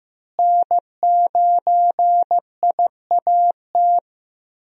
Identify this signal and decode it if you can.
Morse code, heard as a keyed tone, about 15 wpm, 700 Hz. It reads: N9IAT
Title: USA Call Signs - 5 Characters - Repeated 2x - 15wpm